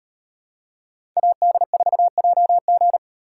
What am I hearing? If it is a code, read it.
Morse code, heard as a keyed tone, about 38 wpm, 700 Hz. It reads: AD4JG